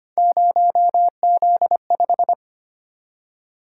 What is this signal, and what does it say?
Morse code, heard as a keyed tone, about 25 wpm, 700 Hz. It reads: 0Z5
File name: Three Letter-Number Combinations - 25wpm